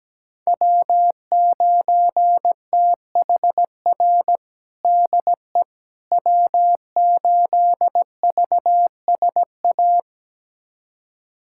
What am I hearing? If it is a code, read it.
Morse code, heard as a keyed tone, about 17 wpm, 700 Hz. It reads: W9THR DE W8VSA
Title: USA Callsign DE Callsign - 5 Characters - 17wpm